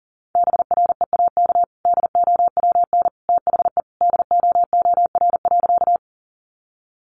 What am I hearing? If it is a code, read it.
Morse code, heard as a keyed tone, about 40 wpm, 700 Hz. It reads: BREAK DOWN THE DOOR.
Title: Sentences from Top 900 Words 40wpm